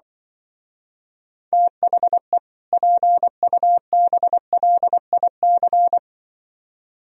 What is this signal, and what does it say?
Morse code, heard as a keyed tone, about 24 wpm, 700 Hz. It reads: THE PUBLIC
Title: Speed-Racing - Sets of 2 Words 20wpm